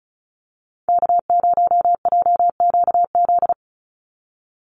Morse code, heard as a keyed tone, about 34 wpm, 700 Hz. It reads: K0JQZ
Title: NA SOTA Chaser Callsigns 35wpm